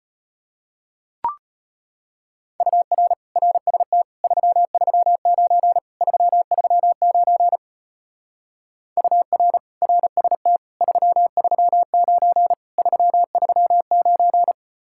Morse code, heard as a keyed tone, about 33 wpm, 700 Hz. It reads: UR RST 339 339 UR RST 339 339